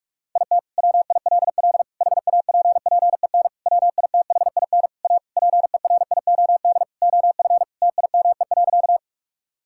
Morse code, heard as a keyed tone, about 46 wpm, 700 Hz. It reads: IT WILL HAPPEN WITHIN A PERIOD OF TIME.
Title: Sentences from Top 1000 Words - Review Part 5 - 45wpm